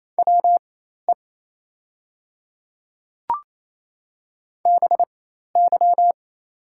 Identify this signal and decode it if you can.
Morse code, heard as a keyed tone, about 28 wpm, spaced as for 13 wpm, 700 Hz. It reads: WE BY
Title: 4x Character Spacing - Top 100 Words - 28wpm